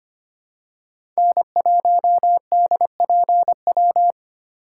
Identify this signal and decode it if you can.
Morse code, heard as a keyed tone, about 25 wpm, 700 Hz. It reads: N1DPW
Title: USA Call Signs - 5 Characters - 2x Character Spacing - 25wpm